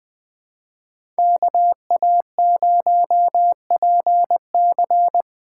Morse code, heard as a keyed tone, about 24 wpm, 700 Hz. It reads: KA0PC
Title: Speed Racing - USA 2x2 Call Signs 20wpm